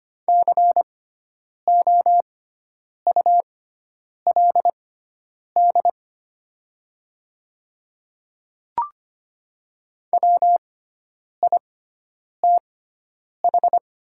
Morse code, heard as a keyed tone, about 25 wpm, spaced as for 9 wpm, 700 Hz. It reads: COULD WITH
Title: 6x Character Spacing - Top 100 Words - 25wpm